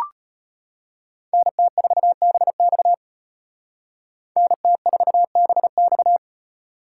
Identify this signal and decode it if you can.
Morse code, heard as a keyed tone, about 32 wpm, 700 Hz. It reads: NT4BX NT4BX